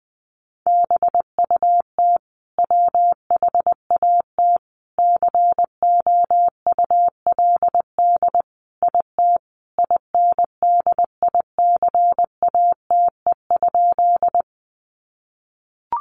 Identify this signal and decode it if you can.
Morse code, heard as a keyed tone, about 20 wpm, 700 Hz. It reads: BUT WHAT COULD IT INDICATE?
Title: Sentences from Top 1300 Words 20wpm